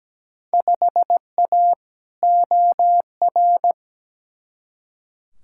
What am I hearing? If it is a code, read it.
Morse code, heard as a keyed tone, about 17 wpm, 700 Hz. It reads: 5A OR
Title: Field Day Exchanges - 17wpm